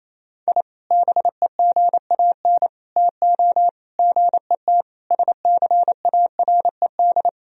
Morse code, heard as a keyed tone, about 28 wpm, 700 Hz. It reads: I BEGAN TO GET SCARED
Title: Sentences from Top 1000 Words - Review Part 4 - 28wpm